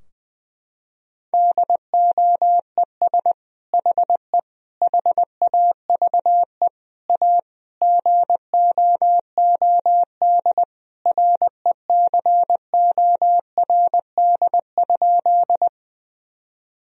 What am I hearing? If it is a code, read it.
Morse code, heard as a keyed tone, about 20 wpm, 700 Hz. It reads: DOES HE HAVE A GOOD RECORD?
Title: Sentences from Top 1000 Words - Review Part 3 - 20wpm